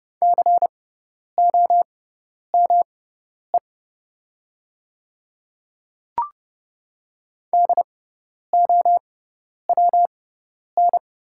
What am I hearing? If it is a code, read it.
Morse code, heard as a keyed tone, about 30 wpm, spaced as for 10 wpm, 690 Hz. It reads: COME DOWN
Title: 6x Character Spacing - Top 100 Words - 30wpm